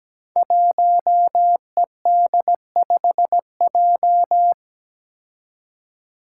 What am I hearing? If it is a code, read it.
Morse code, heard as a keyed tone, about 17 wpm, 700 Hz. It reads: 1ED5J
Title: Five Challenging Letter-Number Combinations 17wpm